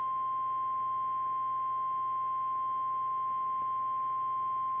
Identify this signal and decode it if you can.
Emergency Alert System (SAME) tones (the 1050 Hz attention tone). It is heard from 0.2 to 4.3 s.
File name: eas / KKPRKXPN / KKPRKXPN